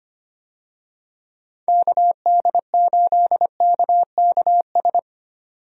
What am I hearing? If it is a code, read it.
Morse code, heard as a keyed tone, about 25 wpm, 700 Hz. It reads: KD8KKS
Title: USA Call Signs - 6 Characters - 25wpm